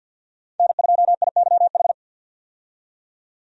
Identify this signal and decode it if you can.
Morse code, heard as a keyed tone, about 49 wpm, 700 Hz. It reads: N2IYH
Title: USA Call Signs - 5 Characters - 50wpm